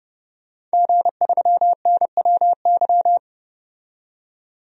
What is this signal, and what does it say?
Morse code, heard as a keyed tone, about 31 wpm, 700 Hz. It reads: G3NWY
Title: DXCC-100 Call Signs - 5 Characters - 30wpm